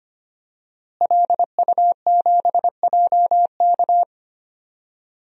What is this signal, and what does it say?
Morse code, heard as a keyed tone, about 25 wpm, 700 Hz. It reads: LU7JK
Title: DXCC-100 Call Signs - 5 Characters - Repeated 2x - 25wpm